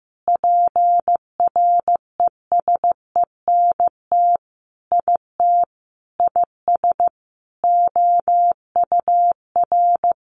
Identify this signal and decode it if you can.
Morse code, heard as a keyed tone, about 15 wpm, 700 Hz. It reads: PRESENT IT IS OUR
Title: Sentences from Top 800 Words 15wpm